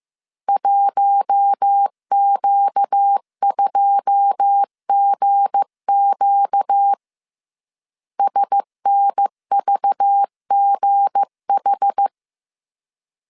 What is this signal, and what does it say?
Morse code, heard as a keyed tone, about 15 wpm, 800 Hz. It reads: 1Q2GQ SNVGH
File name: CW / DET / 1 9 1500C S C 14wpm 3245